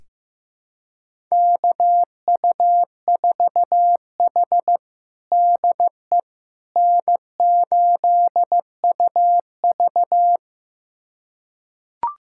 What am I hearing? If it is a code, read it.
Morse code, heard as a keyed tone, about 16 wpm, 700 Hz. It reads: KU4H DE N8UV
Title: USA Callsign DE Callsign - 4 Characters - 15wpm